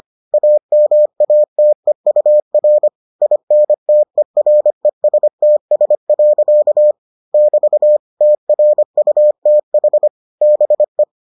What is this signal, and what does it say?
Morse code, heard as a keyed tone, about 25 wpm, 600 Hz. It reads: AMATEUR INTERESTS. <BT> TRUTH BE